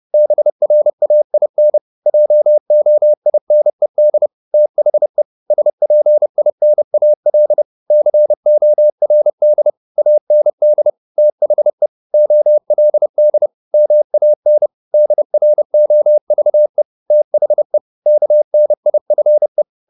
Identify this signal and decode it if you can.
Morse code, heard as a keyed tone, about 30 wpm, 600 Hz. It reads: BRAIN JOINED THE SPINAL CORD AND THE OLD MAN DROVE THE KNIFE